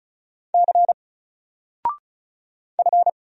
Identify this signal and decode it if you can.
Morse code, heard as a keyed tone, about 36 wpm, 700 Hz. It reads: C F
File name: Single-Letter-Number-35wpm